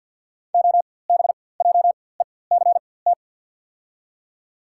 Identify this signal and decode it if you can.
Morse code, heard as a keyed tone, about 53 wpm, spaced as for 24 wpm, 700 Hz. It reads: OBJECT
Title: 4x Character Spacing - Top 500 Words - 50wpm